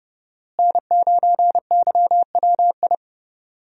Morse code, heard as a keyed tone, about 31 wpm, 700 Hz. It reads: N9YWI
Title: USA Call Signs - 5 Characters - Repeated 3x - 30wpm